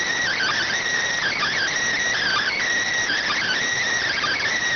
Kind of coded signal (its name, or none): Emergency Alert System (SAME) tones